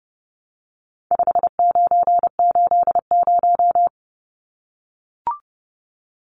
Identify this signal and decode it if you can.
Morse code, heard as a keyed tone, about 30 wpm, 700 Hz. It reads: 5980
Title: Numbers - 4-Digit 30wpm